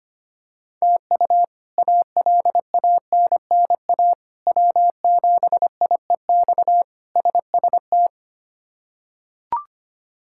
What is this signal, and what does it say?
Morse code, heard as a keyed tone, about 25 wpm, 700 Hz. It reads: TU ALANNA W7IEX SST
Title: K1USN SST QSO Practice - 25wpm